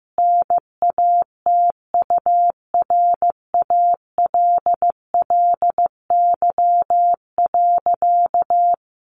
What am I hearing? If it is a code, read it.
Morse code, heard as a keyed tone, about 14 wpm, 700 Hz. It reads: NATURALLY.
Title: Sentences from Top 1300 Words 15wpm